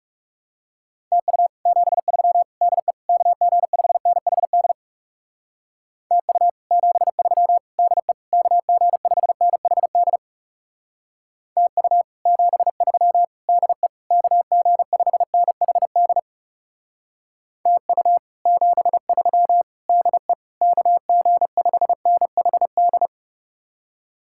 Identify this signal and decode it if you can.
Morse code, heard as a keyed tone, about 36 wpm, 700 Hz. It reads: TU 73 DE KG5NHD TU 73 DE KG5NHD TU 73 DE KG5NHD TU 73 DE KG5NHD
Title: Speed-Racing Common QSO Phrases - 30wpm